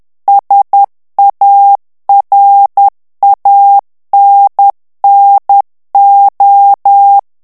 Morse code, heard as a keyed tone, about 10 wpm, 800 Hz. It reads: SARANNO
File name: ITALIEN texte clair 2 vitesse 10 mots minute